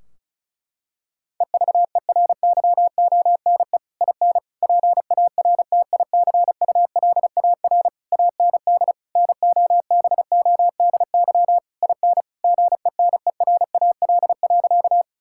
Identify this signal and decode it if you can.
Morse code, heard as a keyed tone, about 36 wpm, 700 Hz. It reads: EVERYONE IN PARTICULAR AND NOBODY IN GENERAL.
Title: Sentences from Top 1600 Words 35wpm